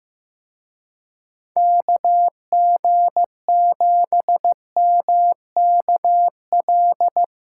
Morse code, heard as a keyed tone, about 17 wpm, 700 Hz. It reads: KG7MKL
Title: Speed Racing - USA Call Signs 15wpm